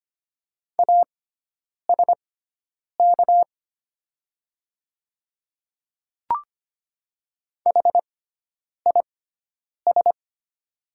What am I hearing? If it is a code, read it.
Morse code, heard as a keyed tone, about 25 wpm, spaced as for 9 wpm, 690 Hz. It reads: ASK HIS
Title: 6x Character Spacing - Top 300 Words - 25wpm